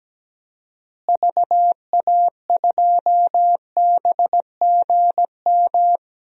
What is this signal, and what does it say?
Morse code, heard as a keyed tone, about 17 wpm, 700 Hz. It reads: VA2BGM
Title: DXCC-100 Call Signs - 6 Characters - Repeated 2x - 17wpm